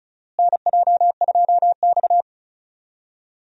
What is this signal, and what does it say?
Morse code, heard as a keyed tone, about 36 wpm, 700 Hz. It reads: NJ2X
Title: USA Call Signs - 4 Characters - Repeated 3x - 35wpm